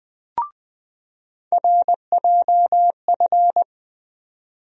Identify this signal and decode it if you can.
Morse code, heard as a keyed tone, about 20 wpm, 700 Hz. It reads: RJF
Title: USA Call Sign Postfixes - 3 Letters 20wpm